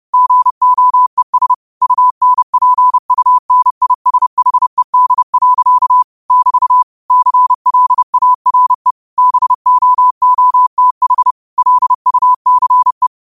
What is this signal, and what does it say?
Morse code, heard as a keyed tone, about 30 wpm, 1000 Hz. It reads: GOES UNPUNISHED. <BT> CLARE BOOTH LUCE